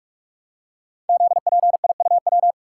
Morse code, heard as a keyed tone, about 44 wpm, 690 Hz. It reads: ZPIUW
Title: Code Groups - 5 Letters - 45wpm